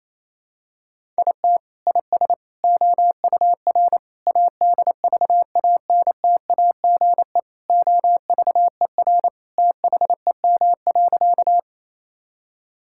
Morse code, heard as a keyed tone, about 27 wpm, 700 Hz. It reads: IT IS OUR ADVANTAGE OVER THEM.
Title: Sentences from Top 1000 Words - Review Part 6 - 28wpm